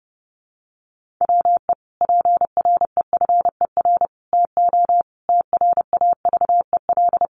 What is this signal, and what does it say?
Morse code, heard as a keyed tone, about 30 wpm, 700 Hz. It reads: WE PREFER TO TRAVEL
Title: Sentences from Top 1100 Words 30wpm